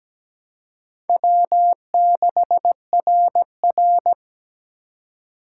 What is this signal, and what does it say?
Morse code, heard as a keyed tone, about 17 wpm, 700 Hz. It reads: W6RR
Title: USA Call Signs - 4 Characters - 4x Character Spacing - 17wpm